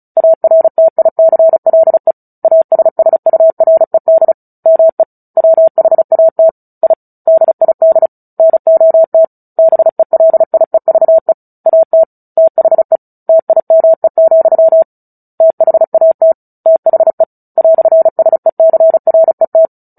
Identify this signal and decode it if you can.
Morse code, heard as a keyed tone, about 34 wpm, 650 Hz. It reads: ARTICLE ASSURED ME WHAT I DID NOT BELIEVE AT THE TIME, THAT THE "SECRET
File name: Book Two / 09-Chapter 9 - 35 WPM